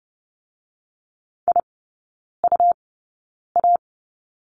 Morse code, heard as a keyed tone, about 30 wpm, spaced as for 9 wpm, 700 Hz. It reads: IUA
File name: Lesson 17e - Letter u - Three Characters 30wpm